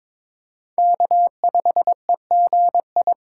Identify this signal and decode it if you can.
Morse code, heard as a keyed tone, about 22 wpm, 700 Hz. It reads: K5EGI
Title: USA Call Signs - 5 Characters - 3x Character Spacing - 22wpm